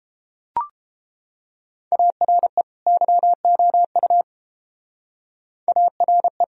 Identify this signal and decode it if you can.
Morse code, heard as a keyed tone, about 30 wpm, 700 Hz. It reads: ARE YOU ARE